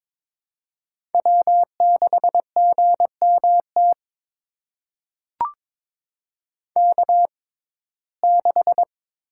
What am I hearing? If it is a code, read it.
Morse code, heard as a keyed tone, about 22 wpm, spaced as for 16 wpm, 700 Hz. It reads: W6GMT K6
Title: USA Call Signs - 5 Characters - 6x Character Spacing - 22wpm